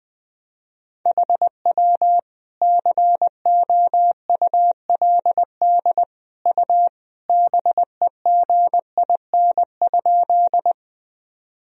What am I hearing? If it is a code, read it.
Morse code, heard as a keyed tone, about 21 wpm, 700 Hz. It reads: HW COULD U BEGIN?
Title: Sentences from Top 300 Words and Abbreviations 20wpm